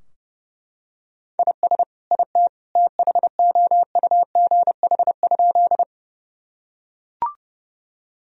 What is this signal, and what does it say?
Morse code, heard as a keyed tone, about 30 wpm, 700 Hz. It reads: IS IT THOUGH?
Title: Sentences from Top 1000 Words - Review Part 2 - 30wpm